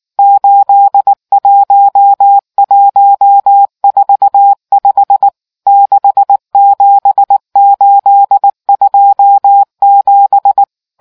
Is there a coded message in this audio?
Yes: Morse code from 0.2 to 10.6 s, heard as a keyed tone, about 19 wpm, 800 Hz. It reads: 81145 67827